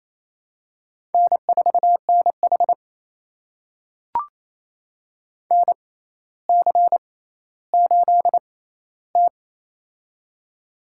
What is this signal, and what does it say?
Morse code, heard as a keyed tone, about 28 wpm, spaced as for 14 wpm, 700 Hz. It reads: N4NH NC8T
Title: USA Call Signs - 4 Characters - 6x Character Spacing - 28wpm